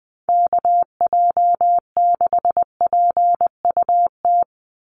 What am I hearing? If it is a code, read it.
Morse code, heard as a keyed tone, about 20 wpm, 700 Hz. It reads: KJ6PUT
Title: USA Call Signs 20wpm